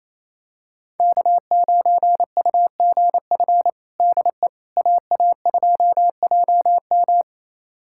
Morse code, heard as a keyed tone, about 28 wpm, 700 Hz. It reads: K9UGF DE AA2JM
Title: USA Callsign DE Callsign - 5 Characters - 28wpm